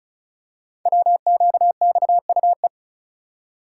Morse code, heard as a keyed tone, about 36 wpm, 700 Hz. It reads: WQXUE
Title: Code Groups - 5 Letters - 35wpm